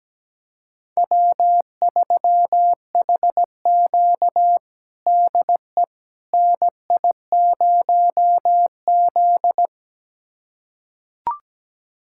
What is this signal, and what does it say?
Morse code, heard as a keyed tone, about 17 wpm, 700 Hz. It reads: W3HQ DE NI0Z
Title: USA Callsign DE Callsign - 4 Characters - 17wpm